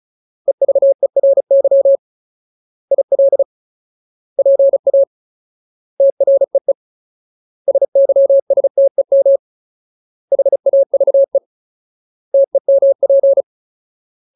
Morse code, heard as a keyed tone, about 34 wpm, 550 Hz. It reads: EVERY IL PA TREE SYSTEM HAVE TEMP
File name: Ultimate Word List / 35WPM Ultimate